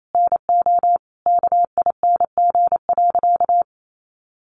Morse code, heard as a keyed tone, about 28 wpm, 700 Hz. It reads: NO KING.
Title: Sentences from Top 600 Words 28wpm